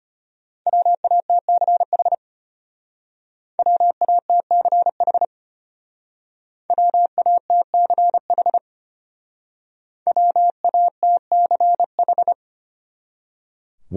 Morse code, heard as a keyed tone, about 33 wpm, 700 Hz. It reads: WATCH WATCH WATCH WATCH